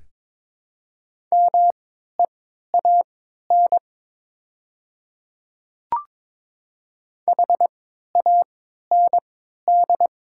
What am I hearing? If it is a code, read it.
Morse code, heard as a keyed tone, about 22 wpm, spaced as for 12 wpm, 700 Hz. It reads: MEAN HAND
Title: 3x Character Spacing - Top 200 Words - 22wpm